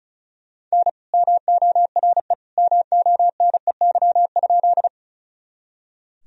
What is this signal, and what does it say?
Morse code, heard as a keyed tone, about 36 wpm, 700 Hz. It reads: N MORE MONEY?
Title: Sentences from Top 400 Words and Abbreviations 35wpm